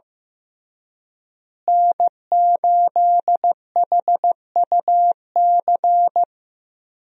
Morse code, heard as a keyed tone, about 15 wpm, 700 Hz. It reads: N8HUC
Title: USA Call Signs - 5 Characters - Repeated 3x - 15wpm